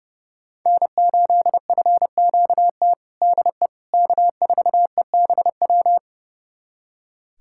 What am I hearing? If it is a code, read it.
Morse code, heard as a keyed tone, about 32 wpm, 700 Hz. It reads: N8FQT DE K4EBW
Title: USA Callsign DE Callsign - 5 Characters - 30wpm